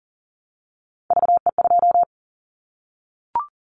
Morse code, heard as a keyed tone, about 40 wpm, 700 Hz. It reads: VE2
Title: Callsign Prefixes for Top 25 DXCC - Repeated 3x - 40wpm